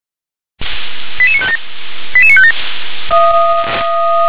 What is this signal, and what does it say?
Five-tone selective call (ZVEI), heard as quick runs of five tones, tones of about 70 ms. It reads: 88957 88957
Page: /357/SDR/ZVEI/88150/2021.09.04/